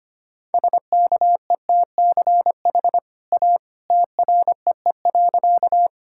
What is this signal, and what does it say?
Morse code, heard as a keyed tone, about 25 wpm, 690 Hz. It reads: SKETCH A TREE.
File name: Easy Sentences with Hints - 25wpm